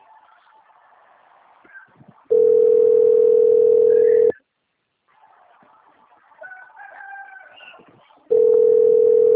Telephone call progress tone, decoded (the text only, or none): ringback tone